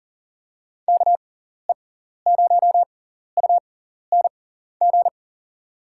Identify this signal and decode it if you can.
Morse code, heard as a keyed tone, about 40 wpm, spaced as for 14 wpm, 700 Hz. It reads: KE0UNG
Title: USA Call Signs - 6 Characters - 6x Character Spacing - 40wpm